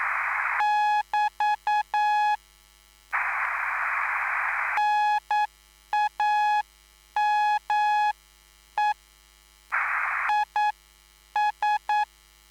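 Morse code, heard as a keyed tone, about 9 wpm, spaced as for 8 wpm, 840 Hz. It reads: <BT> NAME IS
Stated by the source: Live CW QSO between PA2BX and SP3JFK on the 14055 kHz QRS frequency, at a basic 8 wpm.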